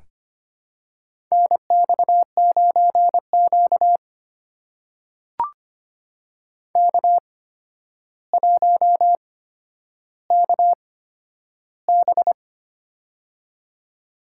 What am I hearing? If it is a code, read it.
Morse code, heard as a keyed tone, about 25 wpm, spaced as for 11 wpm, 700 Hz. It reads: NX9Q K1KB
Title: USA Call Signs - 4 Characters - 8x Character Spacing - 25wpm